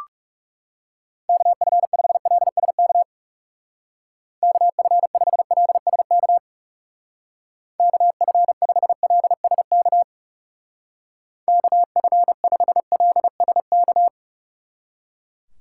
Morse code, heard as a keyed tone, about 35 wpm, 700 Hz. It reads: KF5LSK KF5LSK KF5LSK KF5LSK
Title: Speed Racing - USA 2x3 Call Signs 30wpm